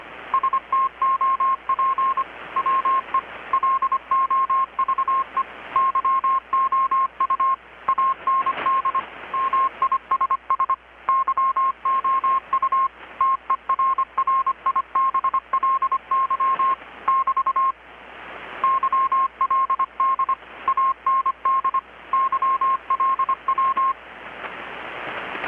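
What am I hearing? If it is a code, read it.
Morse code, heard as a keyed tone, about 25 wpm, 1070 Hz. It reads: STOP WE LOVE YOU AND MISS YOU TERRIBLY <BT> YLD AND YLW
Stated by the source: Morse from KPH's Night of Nights XX maritime radiotelegraphy event, received in Chiapas, Mexico about 3,800 km away on a Kenwood R-600 with a grounded 10 m long-wire antenna.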